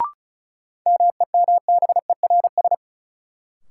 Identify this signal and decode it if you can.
Morse code, heard as a keyed tone, about 36 wpm, 700 Hz. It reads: MEMBERS
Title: Sets of 3 Words - New Word Drill - 35wpm